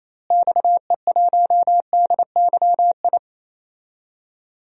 Morse code, heard as a keyed tone, about 28 wpm, 700 Hz. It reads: XE1DYI
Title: DXCC-100 Call Signs - 6 Characters - Repeated 3x - 28wpm